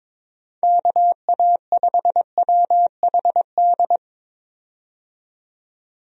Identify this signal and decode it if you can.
Morse code, heard as a keyed tone, about 22 wpm, 700 Hz. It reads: KA5WHD
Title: USA Call Signs - 6 Characters - 22wpm